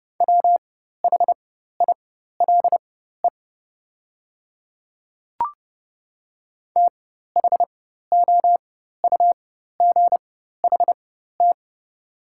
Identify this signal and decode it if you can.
Morse code, heard as a keyed tone, about 32 wpm, spaced as for 14 wpm, 700 Hz. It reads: WHILE THOUGHT
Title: 4x Character Spacing - Top 300 Words - 30wpm